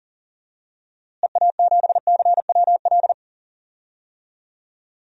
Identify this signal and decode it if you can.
Morse code, heard as a keyed tone, about 40 wpm, 700 Hz. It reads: EA7CWL
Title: DXCC-100 Call Signs - 6 Characters - Repeated 3x - 40wpm